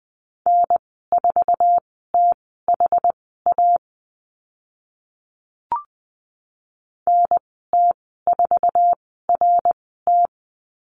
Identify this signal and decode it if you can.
Morse code, heard as a keyed tone, about 20 wpm, spaced as for 14 wpm, 700 Hz. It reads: N4THA NT4RT
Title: 062 Lesson 11g - Letter h - Call Signs 20wpm